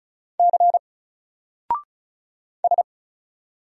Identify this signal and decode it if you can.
Morse code, heard as a keyed tone, about 36 wpm, 700 Hz. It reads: C S
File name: Single-Letter-Number-35wpm